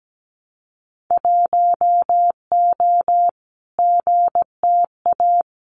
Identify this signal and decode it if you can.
Morse code, heard as a keyed tone, about 17 wpm, 700 Hz. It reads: 1O GTA
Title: Winter Field Day Exchanges 17wpm